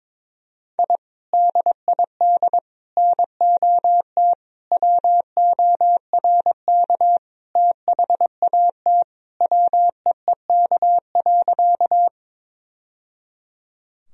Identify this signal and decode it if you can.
Morse code, heard as a keyed tone, about 22 wpm, 700 Hz. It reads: I DID NOT WORK THAT WEEK.
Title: Sentences from Top 1000 Words - Review Part 2 - 22wpm